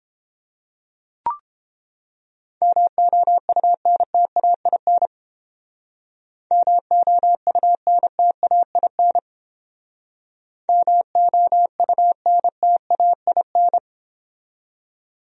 Morse code, heard as a keyed tone, about 31 wpm, 700 Hz. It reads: MOUNTAIN MOUNTAIN MOUNTAIN